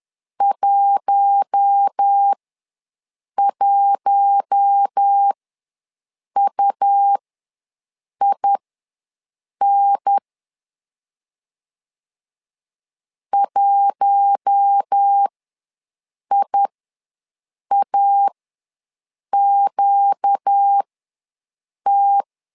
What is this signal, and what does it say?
Morse code, heard as a keyed tone, about 10 wpm, spaced as for 6 wpm, 800 Hz. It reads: 11UIN 1IAQT
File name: LEZIONI / 4 Q1 ws 10wpm 2538